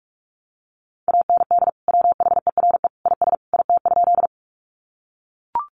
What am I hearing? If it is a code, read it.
Morse code, heard as a keyed tone, about 44 wpm, 700 Hz. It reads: AND WHERE IS IT?
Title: Sentences from Top 200 Words 45wpm